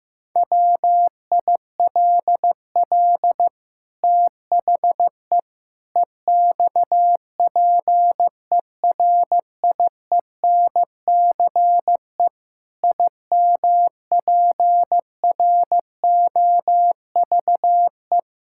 Morse code, heard as a keyed tone, about 15 wpm, 700 Hz. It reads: WILL THE EXPERIENCE IMPROVE
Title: Sentences from Top 1000 Words - Review Part 5 - 15wpm